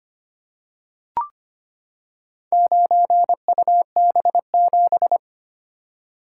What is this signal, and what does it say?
Morse code, heard as a keyed tone, about 25 wpm, 700 Hz. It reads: 9UB7
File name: practice / 25wpm / Four Letter-Number Combinations - 25wpm